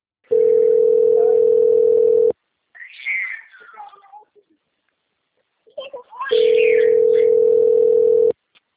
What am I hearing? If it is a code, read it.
Ringback tone (North America), 440 + 480 Hz.